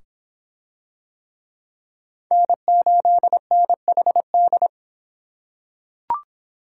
Morse code, heard as a keyed tone, about 24 wpm, 700 Hz. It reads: N8NHD